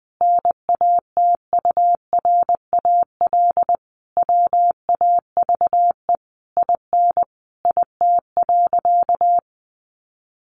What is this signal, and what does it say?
Morse code, heard as a keyed tone, about 20 wpm, 700 Hz. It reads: NATURAL WAVE IN IT.
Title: Sentences from Top 1400 Words 20wpm